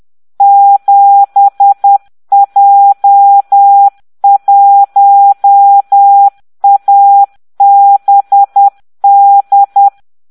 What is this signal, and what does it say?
Morse code, heard as a keyed tone, about 10 wpm, 800 Hz. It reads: 7J1ABD